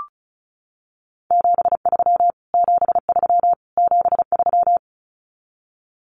Morse code, heard as a keyed tone, about 34 wpm, 700 Hz. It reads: 73 73 73
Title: Abbreviations Repeated 3x - 35wpm